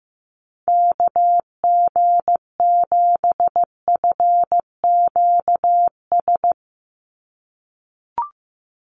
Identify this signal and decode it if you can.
Morse code, heard as a keyed tone, about 15 wpm, 700 Hz. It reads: KG7FQS
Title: USA Call Signs 15wpm